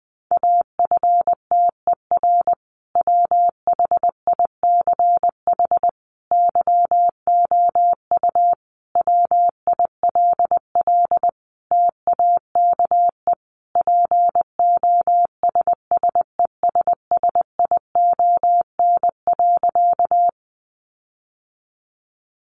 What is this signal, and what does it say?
Morse code, heard as a keyed tone, about 20 wpm, 700 Hz. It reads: AFTER WHICH YOU WILL TAKE POSSESSION.
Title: Sentences from Top 1400 Words 20wpm